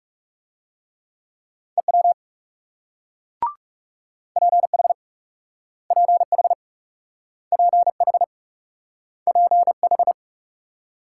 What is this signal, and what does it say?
Morse code, heard as a keyed tone, about 43 wpm, 700 Hz. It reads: EW PH PH PH PH